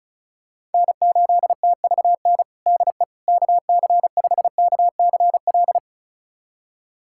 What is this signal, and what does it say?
Morse code, heard as a keyed tone, about 36 wpm, 700 Hz. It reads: N8TVN DE KC5KCL
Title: USA Callsign DE Callsign 35wpm